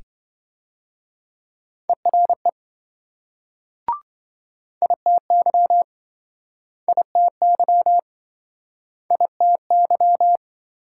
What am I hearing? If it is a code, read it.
Morse code, heard as a keyed tone, about 27 wpm, 700 Hz. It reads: ERE ITY ITY ITY